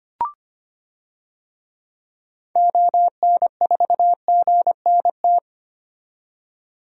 Morse code, heard as a keyed tone, about 25 wpm, 700 Hz. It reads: ON4GNT
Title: DXCC-100 Call Signs - 6 Characters - 25wpm